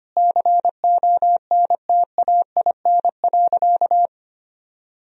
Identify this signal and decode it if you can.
Morse code, heard as a keyed tone, about 25 wpm, 700 Hz. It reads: CONTAIN.
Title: Sentences from Top 1000 Words - Review Part 2 - 25wpm